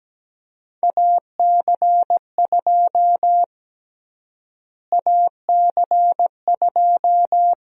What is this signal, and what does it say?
Morse code, heard as a keyed tone, about 17 wpm, 700 Hz. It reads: AC2 AC2
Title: USA Call Sign Prefixes - Repeated 2x - 17wpm